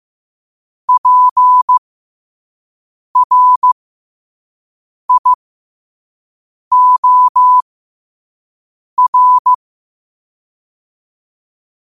Morse code, heard as a keyed tone, about 15 wpm, spaced as for 5 wpm, 1000 Hz. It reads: PRIOR
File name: qotdfiles / 5wpm